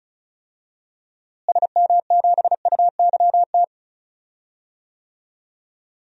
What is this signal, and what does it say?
Morse code, heard as a keyed tone, about 34 wpm, 700 Hz. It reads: SM7UYT
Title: DXCC-100 Call Signs - 6 Characters - Repeated 2x - 35wpm